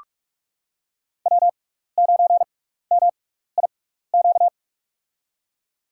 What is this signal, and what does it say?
Morse code, heard as a keyed tone, about 45 wpm, spaced as for 15 wpm, 700 Hz. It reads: W9MIQ
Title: USA Call Signs - 5 Characters - 6x Character Spacing - 45wpm